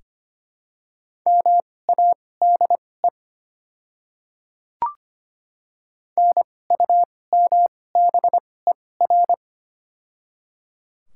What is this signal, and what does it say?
Morse code, heard as a keyed tone, about 25 wpm, spaced as for 18 wpm, 700 Hz. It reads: MADE NUMBER
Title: 2x Character Spacing - Top 200 Words - 25wpm